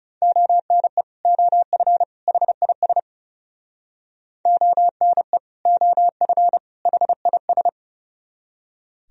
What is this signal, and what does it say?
Morse code, heard as a keyed tone, about 37 wpm, 700 Hz. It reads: ONE OF HIS ONE OF HIS
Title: Speed-Racing - Sets of 3 Words 30wpm